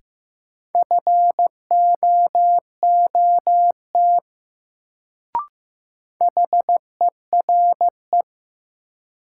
Morse code, heard as a keyed tone, about 15 wpm, 700 Hz. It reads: FOOT HERE
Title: Sets of 5 Words - New Word Drill - 15wpm